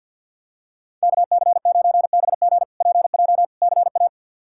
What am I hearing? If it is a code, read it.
Morse code, heard as a keyed tone, about 51 wpm, 700 Hz. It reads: KC9BG PJ CA
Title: K1USN SST QSO Practice - 50wpm